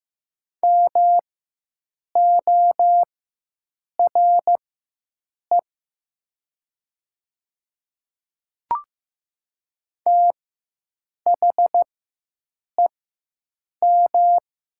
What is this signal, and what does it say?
Morse code, heard as a keyed tone, about 15 wpm, spaced as for 7 wpm, 700 Hz. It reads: MORE THEM
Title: 4x Character Spacing - Top 500 Words - 15wpm